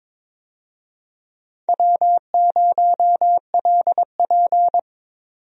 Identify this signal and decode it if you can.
Morse code, heard as a keyed tone, about 22 wpm, 700 Hz. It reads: W0LP
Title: USA Call Signs - 4 Characters - 22wpm